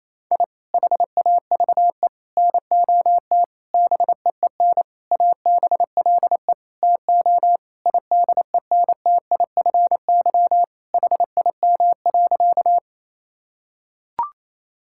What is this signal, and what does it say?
Morse code, heard as a keyed tone, about 28 wpm, 700 Hz. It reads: I HAVE NOT BEEN ABLE TO IDENTIFY HIM.
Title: Sentences from Top 1000 Words - Review Part 6 - 28wpm